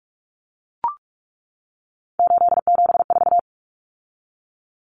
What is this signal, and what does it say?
Morse code, heard as a keyed tone, about 44 wpm, 700 Hz. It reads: 874
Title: Numbers - 3-Digit 45wpm